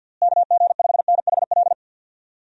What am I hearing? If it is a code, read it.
Morse code, heard as a keyed tone, about 51 wpm, 700 Hz. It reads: KG5NHL
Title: USA Call Signs - 6 Characters - Repeated 2x - 50wpm